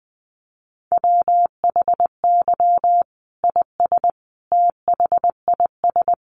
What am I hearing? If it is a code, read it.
Morse code, heard as a keyed tone, about 20 wpm, 700 Hz. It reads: WHY IS THIS
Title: Sentences from Top 1200 Words 20wpm